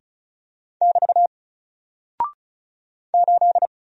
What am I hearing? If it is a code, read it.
Morse code, heard as a keyed tone, about 36 wpm, 700 Hz. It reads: <BT> 8
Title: Single Character (Letter:Number:Punctuation:Prosign) Alt 35wpm